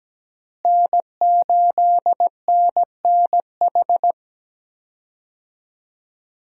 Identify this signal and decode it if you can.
Morse code, heard as a keyed tone, about 17 wpm, 700 Hz. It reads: N8NNH
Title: USA Call Signs - 5 Characters - Repeated 2x - 17wpm